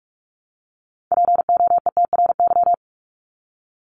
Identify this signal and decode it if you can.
Morse code, heard as a keyed tone, about 44 wpm, 700 Hz. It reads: POETRY
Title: Top 1751-2000 Words 45wpm